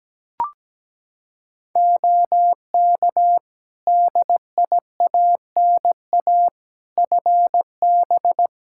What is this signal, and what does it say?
Morse code, heard as a keyed tone, about 17 wpm, 700 Hz. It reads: OK DIANA FB
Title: Practice QSO - Exchange 2 - Info - 17wpm